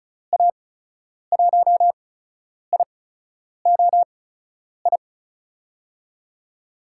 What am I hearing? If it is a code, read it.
Morse code, heard as a keyed tone, about 36 wpm, spaced as for 10 wpm, 700 Hz. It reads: A1IOI
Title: USA Call Signs - 6 Characters - 8x Character Spacing - 35wpm